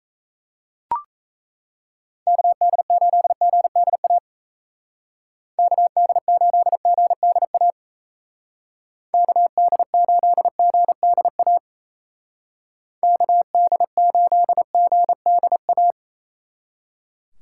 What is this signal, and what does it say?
Morse code, heard as a keyed tone, about 34 wpm, 700 Hz. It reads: KD8GDA KD8GDA KD8GDA KD8GDA
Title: Speed Racing - USA Call Signs 28wpm